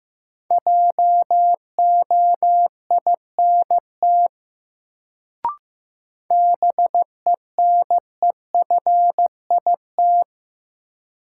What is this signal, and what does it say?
Morse code, heard as a keyed tone, about 15 wpm, 700 Hz. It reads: JOINT BENEFIT
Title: Sets of 5 Words Encore - New Word Drill - 15wpm